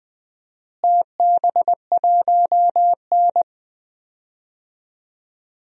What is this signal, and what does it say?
Morse code, heard as a keyed tone, about 20 wpm, 700 Hz. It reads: TB1N
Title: DXCC-100 Call Signs - 4 Characters - Repeated 2x - 20wpm